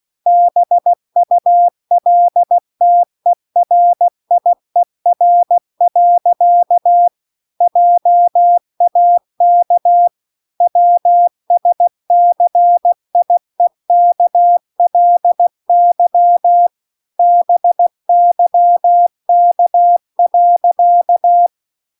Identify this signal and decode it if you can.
Morse code, heard as a keyed tone, about 16 wpm, 700 Hz. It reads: BULTERIER. JAK WSCIEKLY BYK.